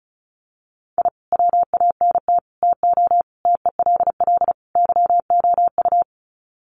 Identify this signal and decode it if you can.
Morse code, heard as a keyed tone, about 34 wpm, 700 Hz. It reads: I WANT TO TELL YOU
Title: Sets of 5 Words Encore 35wpm